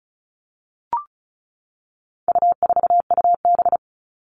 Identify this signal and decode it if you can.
Morse code, heard as a keyed tone, about 34 wpm, 700 Hz. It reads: U4UB
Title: All Cut Numbers - 4-Digit 35wpm